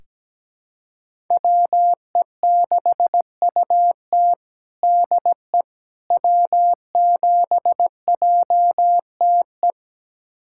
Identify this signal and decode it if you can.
Morse code, heard as a keyed tone, about 17 wpm, 700 Hz. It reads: WE6UT DE W7JTE
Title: USA Callsign DE Callsign - 5 Characters - 17wpm